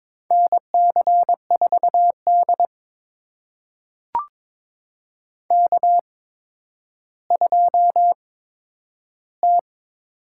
Morse code, heard as a keyed tone, about 22 wpm, spaced as for 11 wpm, 700 Hz. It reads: NC4D K2T
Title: USA Call Signs - 8x Character Spacing - 22wpm